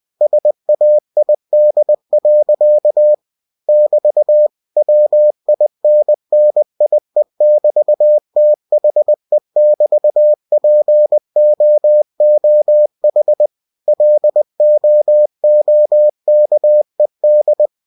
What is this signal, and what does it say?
Morse code, heard as a keyed tone, about 20 wpm, 600 Hz. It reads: SAID. <BT> WINNIE<BT>THE<BT>POOH LOOKED
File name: cw-books / winnie-the-pooh / winnie-20wpm-page-0005